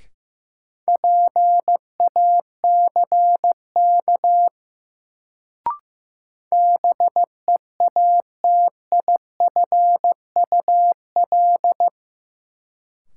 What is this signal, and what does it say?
Morse code, heard as a keyed tone, about 15 wpm, 700 Hz. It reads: PACK BEATIFUL
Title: Sentences from Top 1300 Words - New Word Drill - 15wpm